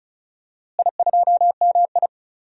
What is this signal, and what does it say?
Morse code, heard as a keyed tone, about 36 wpm, 700 Hz. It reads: I2MI
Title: DXCC-100 Call Signs - 4 Characters - Repeated 2x - 35wpm